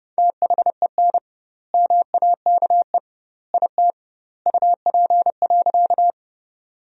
Morse code, heard as a keyed tone, about 30 wpm, 700 Hz. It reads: THEN MAKE IT UP.
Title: 2x Word-Spacing - Sentences from Top 100 Words - 30wpm